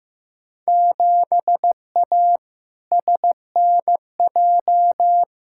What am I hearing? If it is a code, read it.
Morse code, heard as a keyed tone, about 15 wpm, 700 Hz. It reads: 7A SNJ
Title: Field Day Exchanges Encore - 15wpm